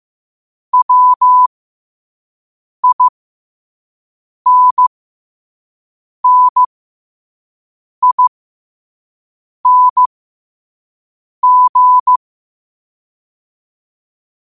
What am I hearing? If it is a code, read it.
Morse code, heard as a keyed tone, about 15 wpm, spaced as for 5 wpm, 1000 Hz. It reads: WINNING
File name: qotdfiles / 5wpm